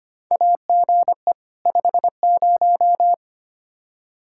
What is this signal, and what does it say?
Morse code, heard as a keyed tone, about 25 wpm, 700 Hz. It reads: AGE 50
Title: Practice QSO - Exchange 3 - More Info - 25wpm